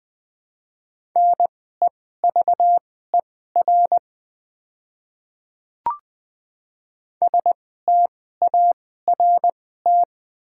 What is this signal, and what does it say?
Morse code, heard as a keyed tone, about 21 wpm, spaced as for 15 wpm, 700 Hz. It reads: NEVER START
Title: 2x Character Spacing - Top 300 Words - 20wpm